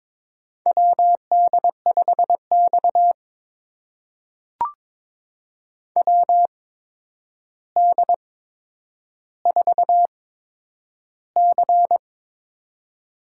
Morse code, heard as a keyed tone, about 22 wpm, spaced as for 9 wpm, 700 Hz. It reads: WD5X WD4C
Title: USA Call Signs - 4 Characters - 8x Character Spacing - 22wpm